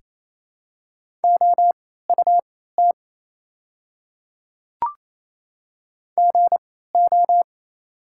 Morse code, heard as a keyed tone, about 28 wpm, spaced as for 16 wpm, 700 Hz. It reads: OUT GO
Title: 3x Character Spacing - Top 100 Words - 28wpm